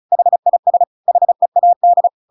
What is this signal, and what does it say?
Morse code, heard as a keyed tone, about 34 wpm, 700 Hz. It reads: HIS HEAD
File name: Pinocchio / Pinocchio ch8 35 35 ik1hge